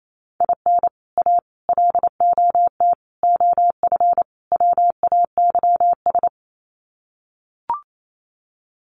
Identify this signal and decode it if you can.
Morse code, heard as a keyed tone, about 28 wpm, 700 Hz. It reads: IN A LOT OF WAYS
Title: Sets of 5 Words Encore 28wpm